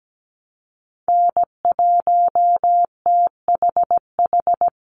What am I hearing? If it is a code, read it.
Morse code, heard as a keyed tone, about 17 wpm, 700 Hz. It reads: N1THH
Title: USA Call Signs 17wpm